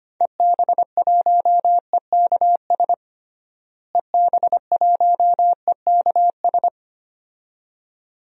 Morse code, heard as a keyed tone, about 25 wpm, 700 Hz. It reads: EB1EKS EB1EKS
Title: DXCC-100 Call Signs - 6 Characters - Repeated 2x - 25wpm